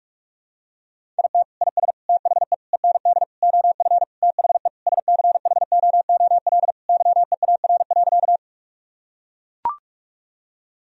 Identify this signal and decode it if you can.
Morse code, heard as a keyed tone, about 41 wpm, 700 Hz. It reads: IT IS THE END OF THE SCHOOL YEAR.
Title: Sentences from Top 1000 Words - Review Part 1 - 45wpm